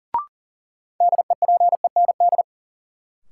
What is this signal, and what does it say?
Morse code, heard as a keyed tone, about 40 wpm, 700 Hz. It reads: DEPEND
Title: Sentences from Top 1300 Words - New Word Drill - 40wpm